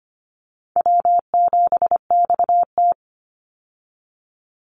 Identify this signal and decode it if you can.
Morse code, heard as a keyed tone, about 25 wpm, 700 Hz. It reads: W7XT
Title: Call Signs Repeated 2x - 25wpm